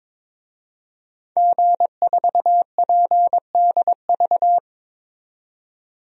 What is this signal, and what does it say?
Morse code, heard as a keyed tone, about 22 wpm, 700 Hz. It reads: G4PDV
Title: DXCC-100 Call Signs - 5 Characters - 22wpm